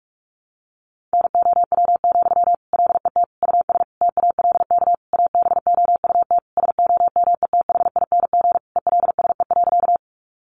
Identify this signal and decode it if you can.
Morse code, heard as a keyed tone, about 44 wpm, 700 Hz. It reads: NOW, LET US TALK ABOUT SOMETHING ELSE.
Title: Sentences from Top 800 Words 45wpm